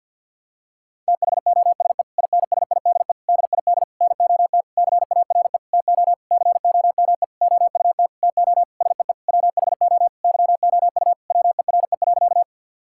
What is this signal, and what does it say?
Morse code, heard as a keyed tone, about 52 wpm, 700 Hz. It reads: THOSE INSIDE DID NOT CARE TO COME OUT TO SEE WHO YOU WERE.
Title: Sentences from Top 1000 Words - Review Part 6 - 50wpm